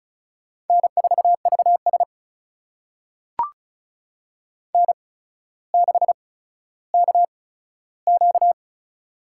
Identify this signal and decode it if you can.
Morse code, heard as a keyed tone, about 35 wpm, spaced as for 15 wpm, 700 Hz. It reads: N4VS N6KQ
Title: USA Call Signs - 4 Characters - 8x Character Spacing - 35wpm